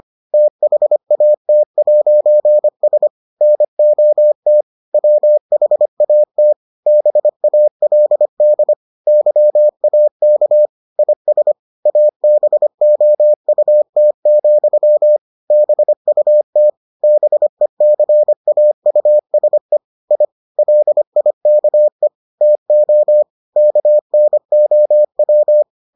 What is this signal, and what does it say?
Morse code, heard as a keyed tone, about 25 wpm, 600 Hz. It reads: THAT'S NOT WHAT BALD YAK IS ABOUT, BUT BECAUSE I LIKE TO KNOW